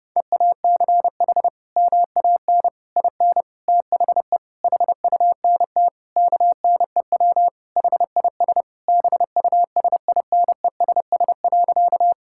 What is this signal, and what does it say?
Morse code, heard as a keyed tone, about 30 wpm, 700 Hz. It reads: EACH MAN IN THE HUNT KNEW HIS BUSINESS.
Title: Sentences from Top 1600 Words 30wpm